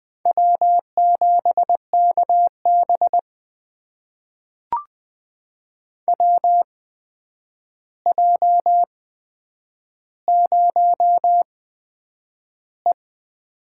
Morse code, heard as a keyed tone, about 20 wpm, spaced as for 9 wpm, 700 Hz. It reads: W7KB WJ0E
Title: USA Call Signs - 4 Characters - 8x Character Spacing - 20wpm